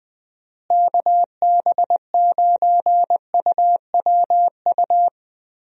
Morse code, heard as a keyed tone, about 21 wpm, 700 Hz. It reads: KB9UWU
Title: USA Call Signs - 6 Characters - Repeated 2x - 20wpm